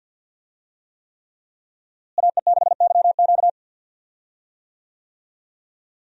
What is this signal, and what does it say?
Morse code, heard as a keyed tone, about 53 wpm, 700 Hz. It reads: AE6YQ
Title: USA Call Signs - 5 Characters - Repeated 3x - 50wpm